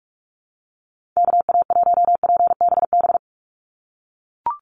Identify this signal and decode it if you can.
Morse code, heard as a keyed tone, about 44 wpm, 700 Hz. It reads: KA1PBB
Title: USA Call Signs 45wpm